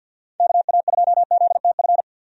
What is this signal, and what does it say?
Morse code, heard as a keyed tone, about 52 wpm, 700 Hz. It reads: KA2ZTF
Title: USA Call Signs - 6 Characters - 50wpm